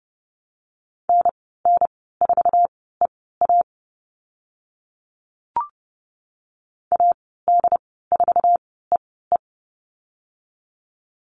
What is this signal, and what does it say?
Morse code, heard as a keyed tone, about 30 wpm, spaced as for 17 wpm, 700 Hz. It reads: NN4EA AD4EE
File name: Lesson 14g - Number 2 - Call Signs 30wpm